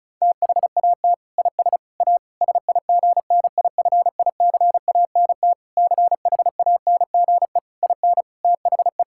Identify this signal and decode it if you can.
Morse code, heard as a keyed tone, about 36 wpm, 700 Hz. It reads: THAT IS A SIGNIFICANT CHANGE IN THE
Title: Sentences from Top 1000 Words - Review Part 6 - 35wpm